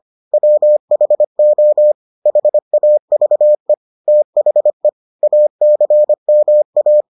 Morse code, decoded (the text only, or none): WHO HAVE THE ACMA